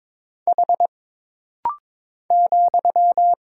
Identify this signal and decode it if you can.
Morse code, heard as a keyed tone, about 22 wpm, 700 Hz. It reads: H ,
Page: This rapid-fire timing is geared to push your ICR (Instant Character Recognition) skills.